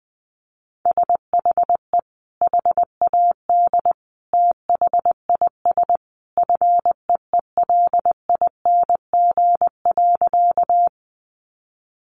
Morse code, heard as a keyed tone, about 20 wpm, 700 Hz. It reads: SHE HAD THIS FEELING.
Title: Sentences from Top 1200 Words 20wpm